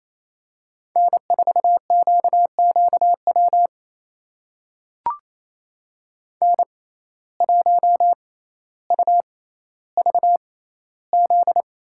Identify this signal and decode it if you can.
Morse code, heard as a keyed tone, about 28 wpm, spaced as for 14 wpm, 700 Hz. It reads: N4QQW N1UVZ
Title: USA Call Signs - 5 Characters - 6x Character Spacing - 28wpm